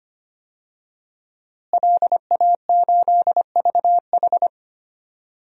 Morse code, heard as a keyed tone, about 25 wpm, 700 Hz. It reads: LA8VH
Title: DXCC-100 Call Signs - 5 Characters - Repeated 2x - 25wpm